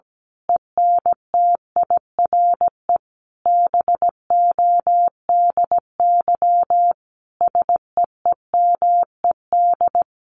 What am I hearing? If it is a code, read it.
Morse code, heard as a keyed tone, about 17 wpm, 700 Hz. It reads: ENTIRE BODY SEEMED
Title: A Sound Of Thunder 17wpm